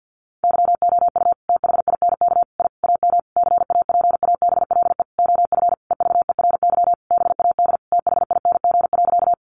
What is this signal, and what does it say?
Morse code, heard as a keyed tone, about 50 wpm, 700 Hz. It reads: YOU THINK I AM CAPABLE OF EVERY BAD THING.
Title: Sentences from Top 1200 Words 50wpm